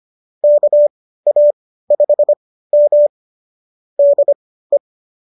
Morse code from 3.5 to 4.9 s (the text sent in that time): DE